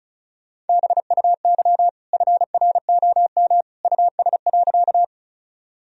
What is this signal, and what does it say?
Morse code, heard as a keyed tone, about 36 wpm, 700 Hz. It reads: BUY FROM US.
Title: Sentences from Top 1000 Words - Review Part 4 - 35wpm